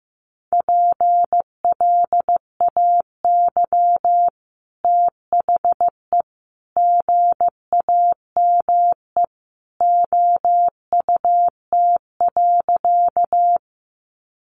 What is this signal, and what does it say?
Morse code, heard as a keyed tone, about 15 wpm, 700 Hz. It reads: PLAY THE GAME OUT.
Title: Sentences from Top 500 Words 15wpm